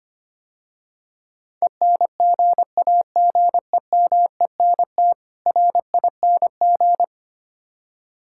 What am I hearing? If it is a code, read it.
Morse code, heard as a keyed tone, about 25 wpm, 700 Hz. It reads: ENGAGEMENT RING
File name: practice / 25wpm / Difficult Words - Speed-Racing with Hints - 25wpm